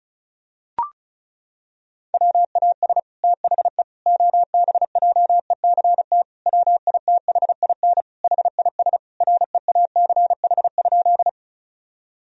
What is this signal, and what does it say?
Morse code, heard as a keyed tone, about 36 wpm, 700 Hz. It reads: WAS THE OBJECT WITHIN HIS REACH?
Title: Sentences from Top 1000 Words - Review Part 4 - 35wpm